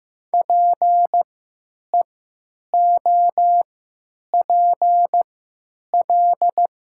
Morse code, heard as a keyed tone, about 15 wpm, spaced as for 9 wpm, 700 Hz. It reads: PEOPL
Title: 3x Character Spacing - Top 300 Words - 15wpm